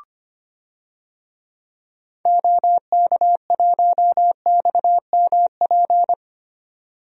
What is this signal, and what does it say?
Morse code, heard as a keyed tone, about 25 wpm, 700 Hz. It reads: OK1XMP
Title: DXCC-100 Call Signs - 6 Characters - 25wpm